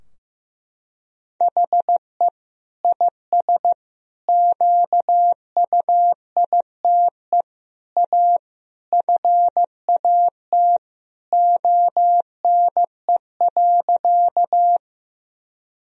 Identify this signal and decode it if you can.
Morse code, heard as a keyed tone, about 15 wpm, 700 Hz. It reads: HE IS QUITE A FAT ONE.
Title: Sentences from Top 1000 Words - Review Part 4 - 15wpm